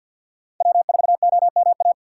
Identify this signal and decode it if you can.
Morse code, heard as a keyed tone, about 48 wpm, 700 Hz. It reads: W4OMA
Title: USA Call Signs - 5 Characters - 6x Character Spacing - 50wpm